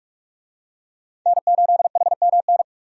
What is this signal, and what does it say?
Morse code, heard as a keyed tone, about 46 wpm, 700 Hz. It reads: N8HMN
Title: USA Call Signs - 5 Characters - 45wpm